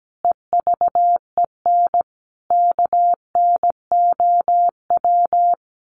Morse code, heard as a keyed tone, about 17 wpm, 700 Hz. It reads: EVEN KNOW
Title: Sets of 5 Words 17wpm